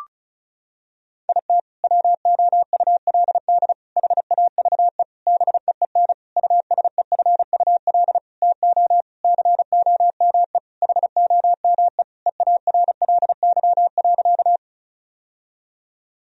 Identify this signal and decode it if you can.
Morse code, heard as a keyed tone, about 34 wpm, 700 Hz. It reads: IT WOULD HAVE BEEN USEFUL TO COME HOME EARLY.
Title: Sentences from Top 1000 Words - Review Part 4 - 35wpm